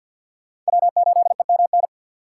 Morse code, heard as a keyed tone, about 62 wpm, 700 Hz. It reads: W8EMN
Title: Speed Racing - USA Call Signs 50wpm